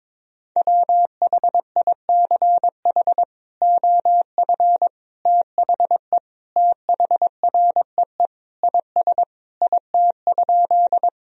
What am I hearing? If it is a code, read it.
Morse code, heard as a keyed tone, about 22 wpm, 700 Hz. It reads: WHICH OF THE THREE IS IT?
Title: Sentences from Top 1000 Words - Review Part 1 - 22wpm